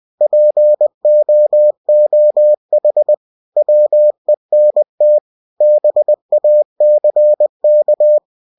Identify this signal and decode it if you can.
Morse code, heard as a keyed tone, about 20 wpm, 600 Hz. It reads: POOH WENT BACK